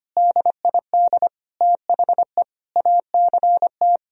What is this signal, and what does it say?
Morse code, heard as a keyed tone, about 25 wpm, 700 Hz. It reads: DID THE ACT
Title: Sentences from Top 1000 Words - Review Part 4 - 25wpm